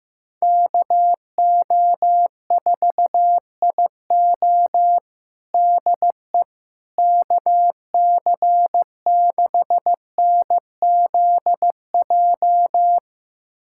Morse code, heard as a keyed tone, about 15 wpm, 700 Hz. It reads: KO4IO DE KC6NZJ
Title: USA Callsign DE Callsign 15wpm